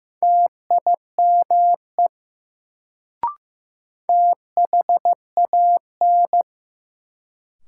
Morse code, heard as a keyed tone, about 15 wpm, 700 Hz. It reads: TIME THAN
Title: Sentences from Top 100 Words - New Word Drill - 15wpm